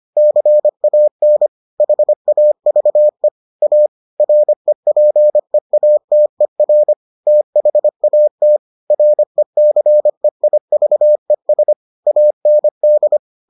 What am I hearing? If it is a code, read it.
Morse code, heard as a keyed tone, about 25 wpm, 600 Hz. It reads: CAN HAVE A REPEATER THAT RECEIVES AND